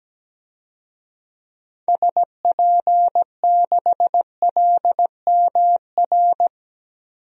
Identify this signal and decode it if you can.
Morse code, heard as a keyed tone, about 17 wpm, 700 Hz. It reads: SP6LMR
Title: DXCC-100 Call Signs - 6 Characters - Repeated 2x - 17wpm